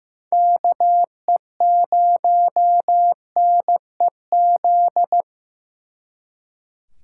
Morse code, heard as a keyed tone, about 18 wpm, 700 Hz. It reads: KE0NEZ
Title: Speed Racing - USA 2x3 Call Signs 15wpm